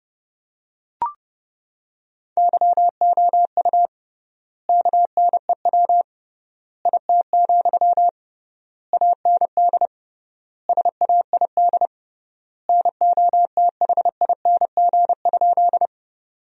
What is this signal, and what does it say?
Morse code, heard as a keyed tone, about 32 wpm, 700 Hz. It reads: YOU KNEW IT, AND SAID NOTHING?
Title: 3x Word-Spacing - Sentences from Top 500 Words - 30wpm